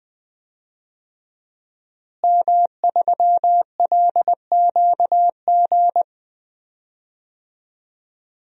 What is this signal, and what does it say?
Morse code, heard as a keyed tone, about 20 wpm, 700 Hz. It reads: M3LQG
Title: DXCC-100 Call Signs - 5 Characters - Repeated 2x - 20wpm